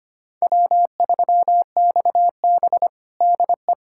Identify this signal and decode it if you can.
Morse code, heard as a keyed tone, about 25 wpm, 700 Hz. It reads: W3XB DE
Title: USA Callsign DE Callsign - 4 Characters - 25wpm